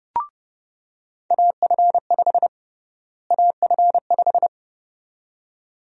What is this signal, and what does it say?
Morse code, heard as a keyed tone, about 32 wpm, 700 Hz. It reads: AF5 AF5
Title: USA Call Sign Prefixes - Repeated 2x - 30wpm